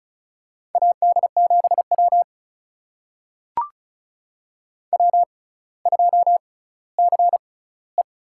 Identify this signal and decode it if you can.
Morse code, heard as a keyed tone, about 36 wpm, spaced as for 18 wpm, 700 Hz. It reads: AD7W W2CE
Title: USA Call Signs - 4 Characters - 6x Character Spacing - 35wpm